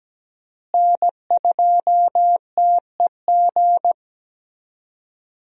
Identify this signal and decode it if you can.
Morse code, heard as a keyed tone, about 17 wpm, 700 Hz. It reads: N2TEG
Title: USA Call Signs - 5 Characters - Repeated 2x - 17wpm